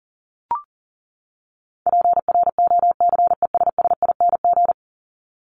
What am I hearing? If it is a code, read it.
Morse code, heard as a keyed tone, about 40 wpm, 700 Hz. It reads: PROCESSING
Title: ING Suffix - Top 100 Words with Base Words 40wpm